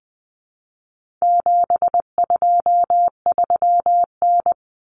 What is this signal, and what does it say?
Morse code, heard as a keyed tone, about 20 wpm, 700 Hz. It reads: 723N
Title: Common Cut Numbers - 4-Digit 20wpm